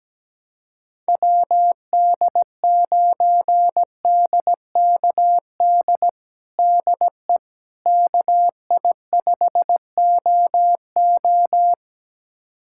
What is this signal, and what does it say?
Morse code, heard as a keyed tone, about 17 wpm, 700 Hz. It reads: WD9DKD DE KI5OO
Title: USA Callsign DE Callsign 17wpm